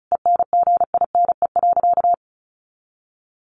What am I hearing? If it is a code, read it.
Morse code, heard as a keyed tone, about 34 wpm, 700 Hz. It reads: ENGINE.
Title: Sentences from Top 1500 Words 35wpm